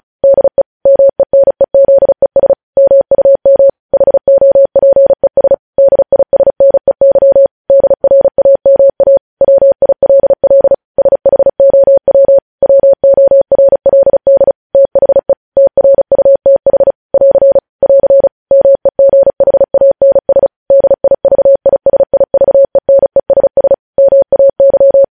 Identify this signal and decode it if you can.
Morse code, heard as a keyed tone, about 36 wpm, 550 Hz. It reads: DE MENEZES MUM HOPES DISNEY DRAMA WILL SHOW WORLD THE TRUTH <AR> <AR> MEGHANS DIVISIVENESS MAY
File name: podcast / News headlines-1x-35x35